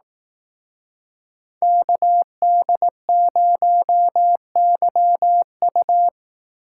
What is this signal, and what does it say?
Morse code, heard as a keyed tone, about 18 wpm, 700 Hz. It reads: KD0YU
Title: Speed Racing - USA 2x2 Call Signs 15wpm